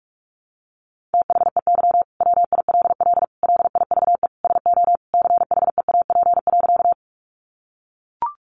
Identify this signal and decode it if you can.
Morse code, heard as a keyed tone, about 44 wpm, 700 Hz. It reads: THEY WILL LIVE SO CHEAP.
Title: Sentences from Top 1200 Words 45wpm